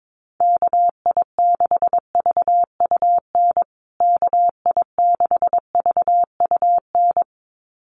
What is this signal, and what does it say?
Morse code, heard as a keyed tone, about 22 wpm, 700 Hz. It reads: KI6VUN KI6VUN
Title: USA Call Signs Repeated 2x 22wpm